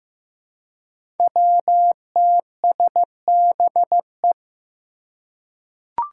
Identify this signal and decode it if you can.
Morse code, heard as a keyed tone, about 15 wpm, 700 Hz. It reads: WTSBE
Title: Five Challenging Letter-Number Combinations 15wpm